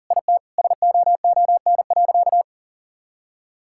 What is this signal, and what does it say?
Morse code, heard as a keyed tone, about 40 wpm, 700 Hz. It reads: IT SOON.
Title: Sentences from Top 1000 Words - Review Part 5 - 40wpm